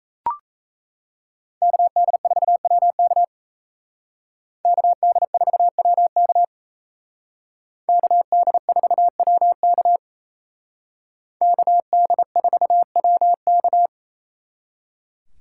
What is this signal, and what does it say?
Morse code, heard as a keyed tone, about 34 wpm, 700 Hz. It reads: KD4WK KD4WK KD4WK KD4WK
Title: Speed Racing - USA 2x2 Call Signs 28wpm